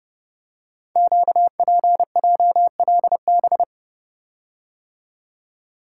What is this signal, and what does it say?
Morse code, heard as a keyed tone, about 30 wpm, 700 Hz. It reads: QPJLB
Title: Code Groups - 5 Letters - 30wpm